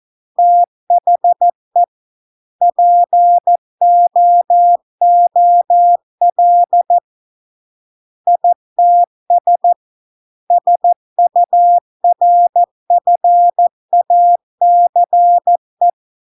Morse code, heard as a keyed tone, about 14 wpm, 700 Hz. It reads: THE POOL ITS SURFACE